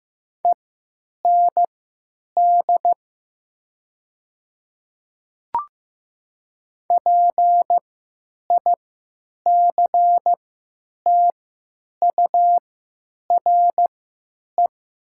Morse code, heard as a keyed tone, about 15 wpm, spaced as for 8 wpm, 700 Hz. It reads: END PICTURE
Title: 3x Character Spacing - Top 500 Words - 15wpm